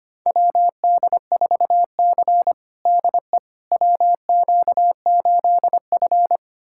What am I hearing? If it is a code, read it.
Morse code, heard as a keyed tone, about 25 wpm, 700 Hz. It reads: WD4C DE WQ8F
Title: USA Callsign DE Callsign - 4 Characters - 25wpm